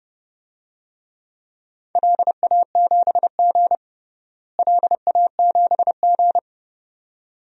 Morse code, heard as a keyed tone, about 30 wpm, 700 Hz. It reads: LA7G LA7G
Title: DXCC-100 Call Signs - 4 Characters - Repeated 2x - 30wpm